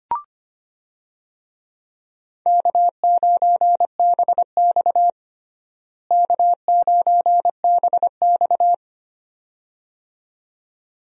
Morse code, heard as a keyed tone, about 25 wpm, 700 Hz. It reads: K9BX K9BX
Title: USA Call Signs - 4 Characters - Repeated 2x - 25wpm